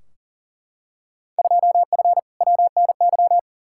Morse code, heard as a keyed tone, about 40 wpm, 700 Hz. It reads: 2F WNY
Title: Field Day Exchanges Encore - 40wpm